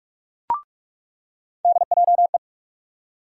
Morse code, heard as a keyed tone, about 46 wpm, 700 Hz. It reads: DJE
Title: USA Call Sign Postfixes - 3 Letters 45wpm